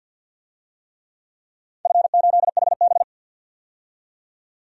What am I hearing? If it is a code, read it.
Morse code, heard as a keyed tone, about 53 wpm, 700 Hz. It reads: F8HB
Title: DXCC-100 Call Signs - 4 Characters - 50wpm